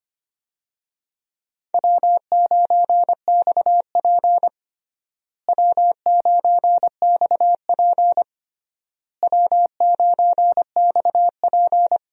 Morse code, heard as a keyed tone, about 25 wpm, 700 Hz. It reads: W9XP W9XP W9XP
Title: USA Call Signs - 4 Characters - Repeated 3x - 25wpm